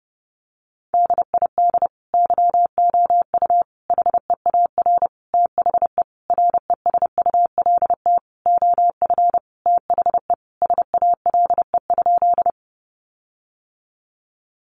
Morse code, heard as a keyed tone, about 30 wpm, 700 Hz. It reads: DID YOU HEAR THE RESULT OF THE SALE?
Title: Sentences from Top 1400 Words 30wpm